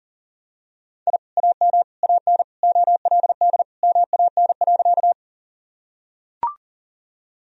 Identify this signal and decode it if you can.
Morse code, heard as a keyed tone, about 40 wpm, 700 Hz. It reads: I AM AN OLD MAN.
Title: Sentences from Top 1000 Words - Review Part 2 - 40wpm